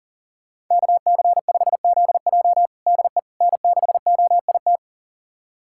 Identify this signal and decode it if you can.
Morse code, heard as a keyed tone, about 40 wpm, 700 Hz. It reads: KC5ZJ DE N6OIT
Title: USA Callsign DE Callsign - 5 Characters - 40wpm